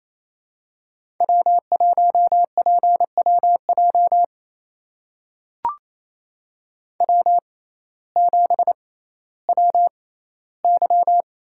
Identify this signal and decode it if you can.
Morse code, heard as a keyed tone, about 28 wpm, spaced as for 15 wpm, 700 Hz. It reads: W1PWJ W7WY
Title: USA Call Signs - 5 Characters - 6x Character Spacing - 28wpm